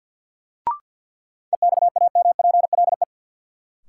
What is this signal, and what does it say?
Morse code, heard as a keyed tone, about 53 wpm, 700 Hz. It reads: EXAMPLE
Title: Sets of 2 Words - New Word Drill - 50wpm